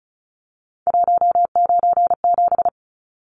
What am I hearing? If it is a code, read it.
Morse code, heard as a keyed tone, about 34 wpm, 700 Hz. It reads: 197
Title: Numbers - 3-Digit 35wpm